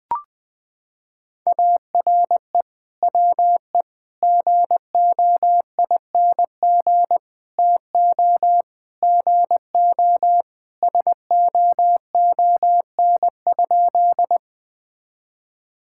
Morse code, heard as a keyed tone, about 21 wpm, 700 Hz. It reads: ARE WE GOING TO GO SOON?
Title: Sentences with 3-Word Phrases - 20wpm